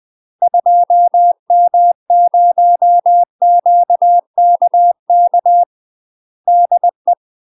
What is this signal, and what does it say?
Morse code, heard as a keyed tone, about 20 wpm, 700 Hz. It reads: 2M0QKK DE
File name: UK MASTER QSOs TRAINING / 20WPM / UK MASTER QSOs 0055